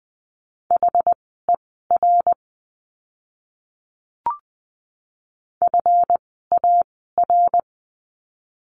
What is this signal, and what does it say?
Morse code, heard as a keyed tone, about 20 wpm, spaced as for 15 wpm, 700 Hz. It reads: HER FAR
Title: 139 Lesson 23f - Letter f - Words 20wpm